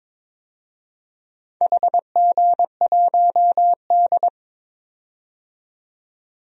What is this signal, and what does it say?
Morse code, heard as a keyed tone, about 22 wpm, 700 Hz. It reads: HG1D
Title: DXCC-100 Call Signs - 4 Characters - 22wpm